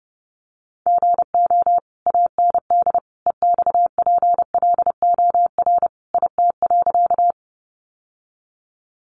Morse code, heard as a keyed tone, about 30 wpm, 700 Hz. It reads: GO AND EXPLOR IT.
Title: Sentences from Top 1100 Words 30wpm